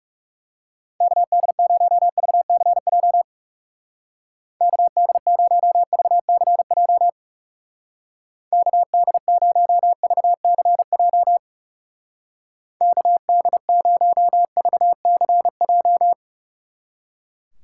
Morse code, heard as a keyed tone, about 37 wpm, 700 Hz. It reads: KD0VCJ KD0VCJ KD0VCJ KD0VCJ
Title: Speed Racing - USA 2x3 Call Signs 30wpm